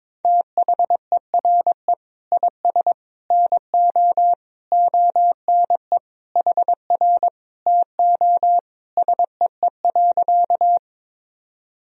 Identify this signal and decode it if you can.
Morse code, heard as a keyed tone, about 22 wpm, 700 Hz. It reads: THERE IS NO ONE HR TO SEE.
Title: Sentences from Top 200 Words and Abbreviations 22wpm